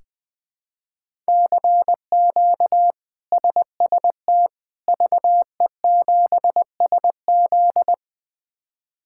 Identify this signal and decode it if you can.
Morse code, heard as a keyed tone, about 21 wpm, 700 Hz. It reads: CQ SST VE7SZ
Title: K1USN SST QSO Practice - 20wpm